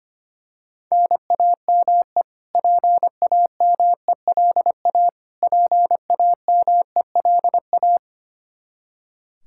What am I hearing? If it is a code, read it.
Morse code, heard as a keyed tone, about 25 wpm, 700 Hz. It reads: NAME PAMELA PAMELA
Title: Practice QSO - Exchange 1 - Beginning - 25wpm